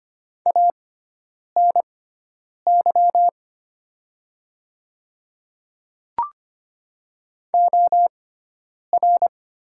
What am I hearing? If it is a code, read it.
Morse code, heard as a keyed tone, about 25 wpm, spaced as for 9 wpm, 700 Hz. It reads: ANY OR
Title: 6x Character Spacing - Top 200 Words - 25wpm